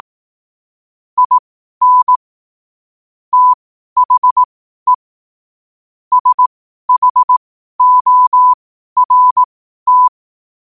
Morse code, heard as a keyed tone, about 18 wpm, spaced as for 12 wpm, 1000 Hz. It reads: IN THE SHORT
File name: qotdfiles / 12wpm